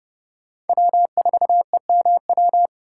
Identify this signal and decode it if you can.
Morse code, heard as a keyed tone, about 28 wpm, 700 Hz. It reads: W4EMW